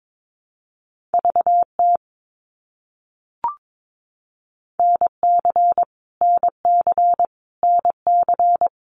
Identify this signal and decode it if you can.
Morse code, heard as a keyed tone, about 22 wpm, 700 Hz. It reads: VT NC NC NC
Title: US States Repeated 3x 22wpm